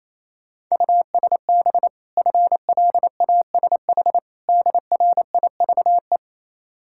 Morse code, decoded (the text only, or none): USB FLASH DRIVE